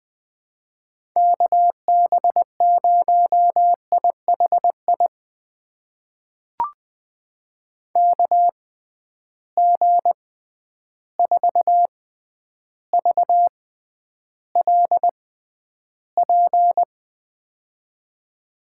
Morse code, heard as a keyed tone, about 20 wpm, spaced as for 10 wpm, 700 Hz. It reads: KB0IHI KG4VLP
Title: USA Call Signs - 6 Characters - 6x Character Spacing - 20wpm